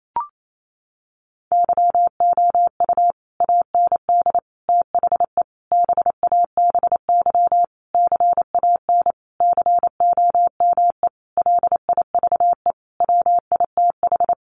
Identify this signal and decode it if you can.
Morse code, heard as a keyed tone, about 28 wpm, 700 Hz. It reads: YOU AND THE BABY CAN COME LIVE WITH
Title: Sentences from Top 1300 Words 28wpm